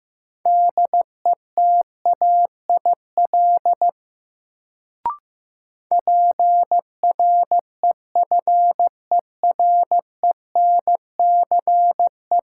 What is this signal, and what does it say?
Morse code, heard as a keyed tone, about 15 wpm, 700 Hz. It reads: DETAIL PREFERENCE
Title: Sentences from Top 1500 Words - New Word Drill Alt - 15wpm